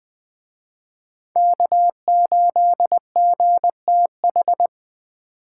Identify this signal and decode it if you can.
Morse code, heard as a keyed tone, about 21 wpm, 700 Hz. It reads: K8GTH
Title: Speed Racing - USA 1x3 Call Signs 17wpm